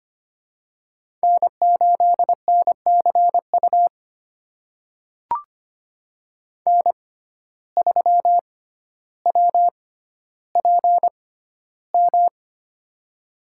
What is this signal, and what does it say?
Morse code, heard as a keyed tone, about 25 wpm, spaced as for 13 wpm, 700 Hz. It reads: N8NCU N3WPM
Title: USA Call Signs - 5 Characters - 6x Character Spacing - 25wpm